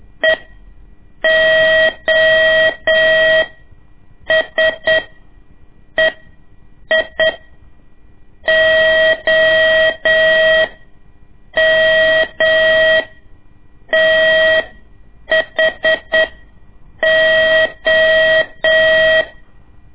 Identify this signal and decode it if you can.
Morse code, heard as a keyed tone, about 8 wpm, spaced as for 6 wpm, 640 Hz. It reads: EOSEI OMTHO
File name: ik1weg2 / CORSI CW / lez2-1